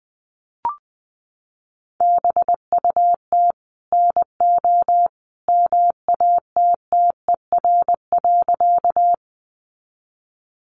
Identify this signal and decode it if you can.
Morse code, heard as a keyed tone, about 20 wpm, 700 Hz. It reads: BUT NO MATTER.
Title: Sentences from Top 700 Words 20wpm